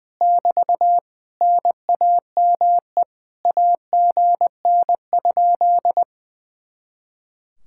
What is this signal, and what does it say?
Morse code, heard as a keyed tone, about 19 wpm, 700 Hz. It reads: <BT> NAME AGN?
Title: Speed-Racing Common QSO Phrases - 20wpm